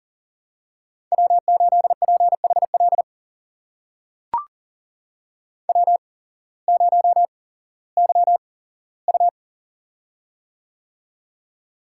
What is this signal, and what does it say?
Morse code, heard as a keyed tone, about 40 wpm, spaced as for 19 wpm, 700 Hz. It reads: W8PHL W0YU
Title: USA Call Signs - 8x Character Spacing - 40wpm